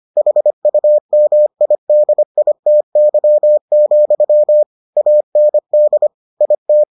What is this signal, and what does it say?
Morse code, heard as a keyed tone, about 25 wpm, 600 Hz. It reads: HUMIDITY, AND IT